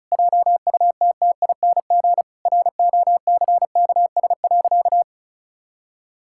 Morse code, heard as a keyed tone, about 34 wpm, 700 Hz. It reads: JUTTING ROCKS.
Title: Pangram Challenge v2 - 35wpm